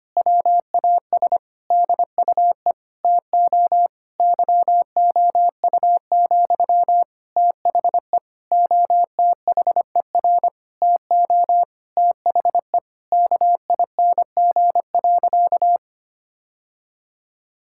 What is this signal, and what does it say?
Morse code, heard as a keyed tone, about 25 wpm, 700 Hz. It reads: WAS DUE TO YOU, THE OTHER TO THE KING.
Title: Sentences from Top 1000 Words - Review Part 6 - 25wpm